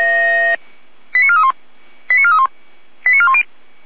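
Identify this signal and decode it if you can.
Five-tone selective call (ZVEI), heard as quick runs of five tones, tones of about 70 ms. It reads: 89431 89431 89410
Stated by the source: ZVEI five-tone call received on SDR, Feuerwehr Orlamünde.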